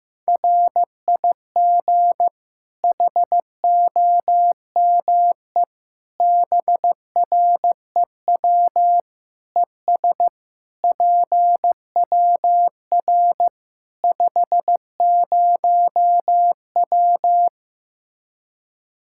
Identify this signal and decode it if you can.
Morse code, heard as a keyed tone, about 16 wpm, 700 Hz. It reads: RIG HOME BREW ES PWR 50W
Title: Practice QSO - Exchange 2 - Info - 15wpm